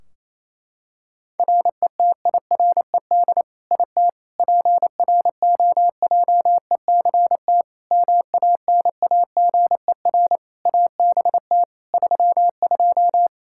Morse code, heard as a keyed tone, about 28 wpm, 700 Hz. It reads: RETIRED IT PROJECT MANAGER ABT 32
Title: Practice QSO - Exchange 3 - More Info - 28wpm